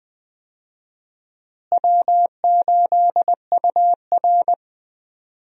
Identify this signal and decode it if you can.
Morse code, heard as a keyed tone, about 20 wpm, 700 Hz. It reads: W8UR
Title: USA Call Signs - 4 Characters - Repeated 2x - 20wpm